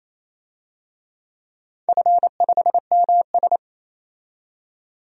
Morse code, heard as a keyed tone, about 28 wpm, 700 Hz. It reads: F5MS
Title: DXCC-100 Call Signs - 4 Characters - 28wpm